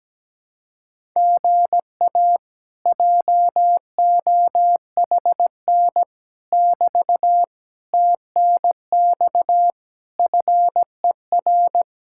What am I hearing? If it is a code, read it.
Morse code, heard as a keyed tone, about 17 wpm, 700 Hz. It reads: GA JOHN <BT> TNX FER
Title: Speed-Racing Common QSO Phrases - 17wpm